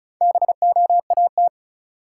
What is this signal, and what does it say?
Morse code, heard as a keyed tone, about 36 wpm, 700 Hz. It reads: BOAT
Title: Sentences from Top 500 Words - New Word Drill - 35wpm